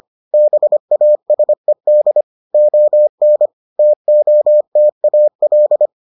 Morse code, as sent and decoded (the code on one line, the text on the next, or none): -... .- ... -.. / --- -. / - --- - .- .-..
BASED ON TOTAL